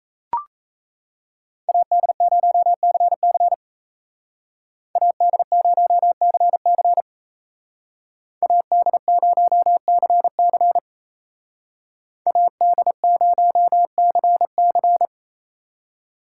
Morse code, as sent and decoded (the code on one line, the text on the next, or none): .- -.. ----- -.-. -.-. / .- -.. ----- -.-. -.-. / .- -.. ----- -.-. -.-. / .- -.. ----- -.-. -.-.
AD0CC AD0CC AD0CC AD0CC